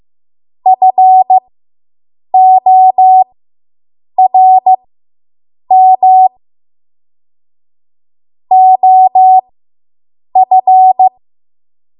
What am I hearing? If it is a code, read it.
Morse code, heard as a keyed tone, about 15 wpm, spaced as for 7 wpm, 750 Hz. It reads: FORM OF